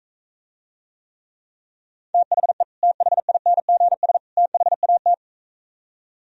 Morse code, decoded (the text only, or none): THE THINGS THAT